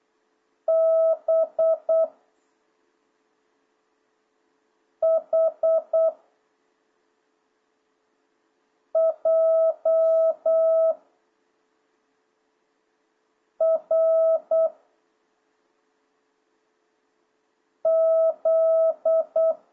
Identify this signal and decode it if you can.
Morse code, heard as a keyed tone, about 8 wpm, 660 Hz. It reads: B H J R Z